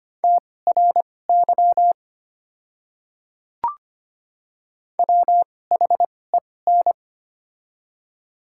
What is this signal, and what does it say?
Morse code, heard as a keyed tone, about 25 wpm, spaced as for 18 wpm, 700 Hz. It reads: TRY WHEN
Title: 2x Character Spacing - Top 500 Words - 25wpm